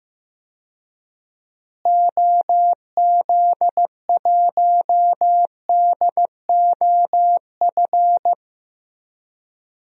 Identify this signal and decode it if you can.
Morse code, heard as a keyed tone, about 15 wpm, 700 Hz. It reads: OZ1DOF
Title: DXCC-100 Call Signs - 6 Characters - Repeated 2x - 15wpm